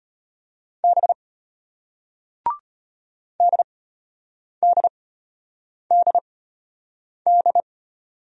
Morse code, heard as a keyed tone, about 33 wpm, spaced as for 8 wpm, 700 Hz. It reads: B DDDD